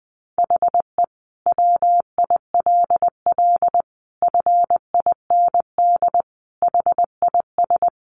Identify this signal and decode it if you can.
Morse code, heard as a keyed tone, about 20 wpm, 700 Hz. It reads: HE WILL FIND HIS
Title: Sentences from Top 1400 Words 20wpm